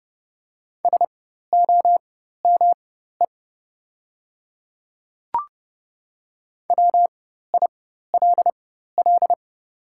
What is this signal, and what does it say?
Morse code, heard as a keyed tone, about 30 wpm, spaced as for 14 wpm, 700 Hz. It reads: SOME WILL
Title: 4x Character Spacing - Top 100 Words - 30wpm